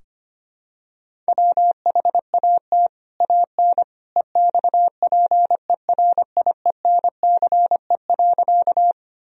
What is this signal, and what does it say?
Morse code, heard as a keyed tone, about 25 wpm, 700 Hz. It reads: WHAT AN EXPERIENCE.
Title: Sentences from Top 1000 Words - Review Part 3 - 25wpm